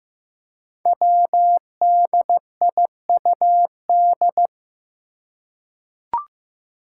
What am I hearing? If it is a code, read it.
Morse code, heard as a keyed tone, about 15 wpm, 700 Hz. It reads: WDIUD
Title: Code Groups - 5 Letter:Numbers - 15wpm